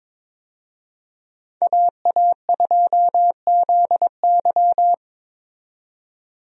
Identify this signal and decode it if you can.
Morse code, heard as a keyed tone, about 22 wpm, 700 Hz. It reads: AA2ZY
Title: USA Call Signs - 5 Characters - 22wpm